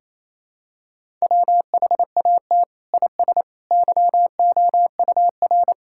Morse code, heard as a keyed tone, about 27 wpm, 700 Hz. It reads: WHAT IS YOUR
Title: Sentences from Top 1000 Words - Review Part 6 - 28wpm